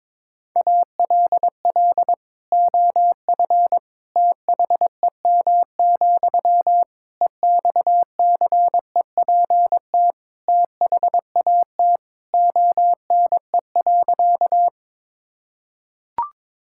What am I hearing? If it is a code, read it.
Morse code, heard as a keyed tone, about 22 wpm, 700 Hz. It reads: ALL OF THEM, EXCEPT THAT ONE.
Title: Sentences from Top 1600 Words 22wpm